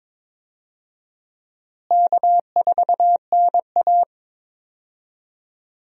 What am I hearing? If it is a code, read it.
Morse code, heard as a keyed tone, about 22 wpm, 700 Hz. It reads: K4NA
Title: USA Call Signs - 4 Characters - Repeated 2x - 22wpm